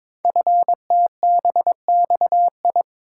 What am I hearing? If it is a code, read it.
Morse code, heard as a keyed tone, about 22 wpm, 700 Hz. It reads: FTBXI
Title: Code Groups - 5 Letters - 22wpm